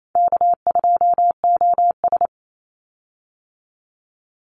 Morse code, heard as a keyed tone, about 28 wpm, 700 Hz. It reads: K2OS
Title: Call Signs Repeated 2x - 28wpm